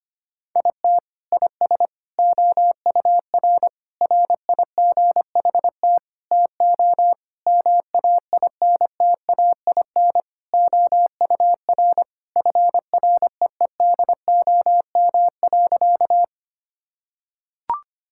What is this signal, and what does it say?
Morse code, heard as a keyed tone, about 25 wpm, 700 Hz. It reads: IT IS OUR RIGHT TO MAINTAIN OUR FREEDOM.
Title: Sentences from Top 1000 Words - Review Part 5 - 25wpm